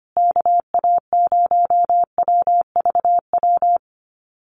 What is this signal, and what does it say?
Morse code, heard as a keyed tone, about 25 wpm, 700 Hz. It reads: KA0WVW
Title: USA Call Signs Repeated 3x 25wpm